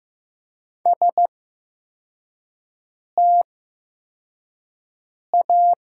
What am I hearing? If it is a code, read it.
Morse code, heard as a keyed tone, about 15 wpm, spaced as for 4 wpm, 700 Hz. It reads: STA
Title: 8x Character Spacing - Top 500 Words - 15wpm